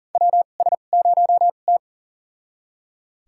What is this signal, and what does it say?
Morse code, heard as a keyed tone, about 40 wpm, spaced as for 29 wpm, 700 Hz. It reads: WS0T
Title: USA Call Signs - 4 Characters - 2x Character Spacing - 40wpm